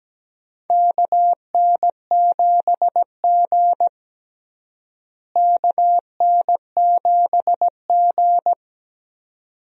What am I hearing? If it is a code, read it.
Morse code, heard as a keyed tone, about 17 wpm, 700 Hz. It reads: KN7G KN7G
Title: USA Call Signs - 4 Characters - Repeated 2x - 17wpm